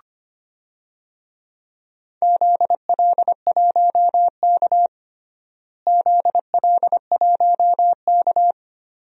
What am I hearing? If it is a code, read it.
Morse code, heard as a keyed tone, about 25 wpm, 700 Hz. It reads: ZL1K ZL1K
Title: DXCC-100 Call Signs - 4 Characters - Repeated 2x - 25wpm